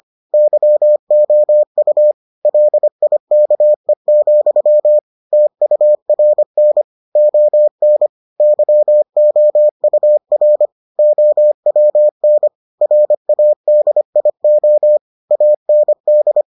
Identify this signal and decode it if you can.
Morse code, heard as a keyed tone, about 25 wpm, 600 Hz. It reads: YOU LIKE, TURN ON YOUR OWN RADIO AND